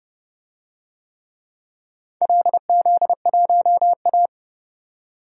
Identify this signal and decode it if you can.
Morse code, heard as a keyed tone, about 31 wpm, 700 Hz. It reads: LZ1A
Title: DXCC-100 Call Signs - 4 Characters - 30wpm